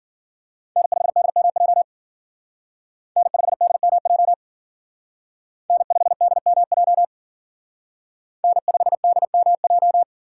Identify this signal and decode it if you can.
Morse code, heard as a keyed tone, about 51 wpm, 700 Hz. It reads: N5DMJ N5DMJ N5DMJ N5DMJ